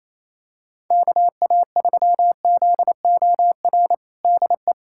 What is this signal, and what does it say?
Morse code, heard as a keyed tone, about 28 wpm, 700 Hz. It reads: KA3ZOR DE
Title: USA Callsign DE Callsign - 6 Characters - 28wpm